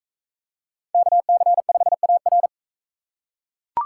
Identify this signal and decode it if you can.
Morse code, heard as a keyed tone, about 42 wpm, 700 Hz. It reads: KC5AR